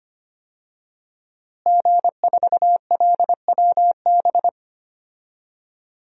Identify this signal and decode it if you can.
Morse code, heard as a keyed tone, about 25 wpm, 700 Hz. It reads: G4LWB
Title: DXCC-100 Call Signs - 5 Characters - Repeated 3x - 25wpm